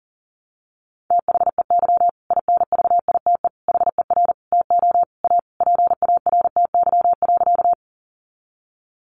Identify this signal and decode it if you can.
Morse code, heard as a keyed tone, about 37 wpm, 700 Hz. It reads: THEY INVITE HER TO A PARTY.
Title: Easy Phrases 40wpm